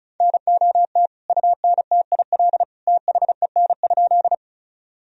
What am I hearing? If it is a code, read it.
Morse code, heard as a keyed tone, about 36 wpm, 700 Hz. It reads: NOT UNTIL THEN?
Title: Sentences from Top 1000 Words - Review Part 1 - 35wpm